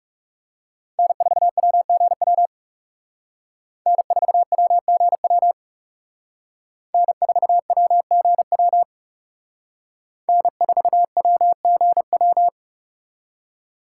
Morse code, heard as a keyed tone, about 34 wpm, 700 Hz. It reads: N4WGW N4WGW N4WGW N4WGW
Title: Speed Racing - USA 1x3 Call Signs 30wpm